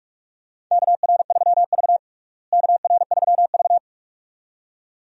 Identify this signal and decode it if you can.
Morse code, heard as a keyed tone, about 46 wpm, 700 Hz. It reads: KR3V KR3V
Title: USA Call Signs - 4 Characters - Repeated 2x - 45wpm